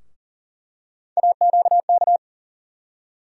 Morse code, heard as a keyed tone, about 40 wpm, 700 Hz. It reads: AQK
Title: USA Call Sign Postfixes - 3 Letters 40wpm